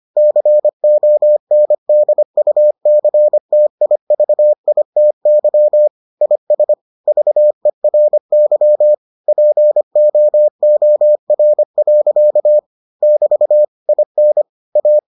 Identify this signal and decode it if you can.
Morse code, heard as a keyed tone, about 25 wpm, 600 Hz. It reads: CONDUCTIVITY IS VERY POOR. <BT> IN A